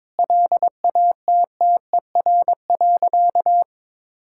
Morse code, heard as a keyed tone, about 22 wpm, 700 Hz. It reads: LATTER.
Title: Sentences from Top 1600 Words 22wpm